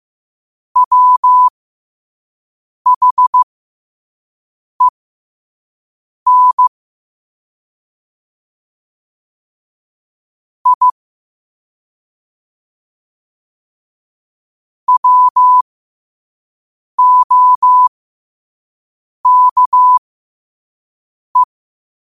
Morse code, heard as a keyed tone, about 15 wpm, spaced as for 5 wpm, 1000 Hz. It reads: WHEN I WOKE
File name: qotdfiles / 5wpm